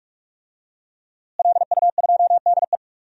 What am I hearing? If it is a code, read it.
Morse code, heard as a keyed tone, about 41 wpm, 700 Hz. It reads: LU2DE
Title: DXCC-100 Call Signs - 5 Characters - Repeated 3x - 45wpm